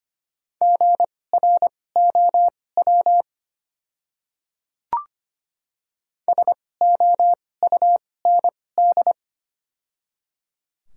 Morse code, heard as a keyed tone, about 25 wpm, spaced as for 18 wpm, 700 Hz. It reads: GROW SOUND
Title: 2x Character Spacing - Top 300 Words - 25wpm